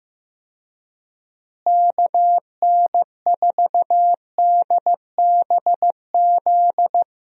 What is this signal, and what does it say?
Morse code, heard as a keyed tone, about 15 wpm, 700 Hz. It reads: KN4DBZ
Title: USA Call Signs - 6 Characters - Repeated 2x - 15wpm